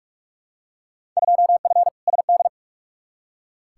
Morse code, heard as a keyed tone, about 46 wpm, 700 Hz. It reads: 2F SD
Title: Field Day Exchanges Encore - 45wpm